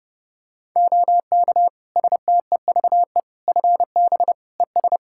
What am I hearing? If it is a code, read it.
Morse code, heard as a keyed tone, about 30 wpm, 700 Hz. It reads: OK STEVE FB ES
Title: Practice QSO - Exchange 2 - Info - 30wpm